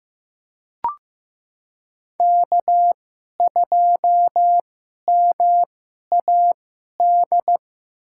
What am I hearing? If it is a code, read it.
Morse code, heard as a keyed tone, about 15 wpm, spaced as for 11 wpm, 700 Hz. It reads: K2MAD
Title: USA Call Signs - 2x Character Spacing - 15wpm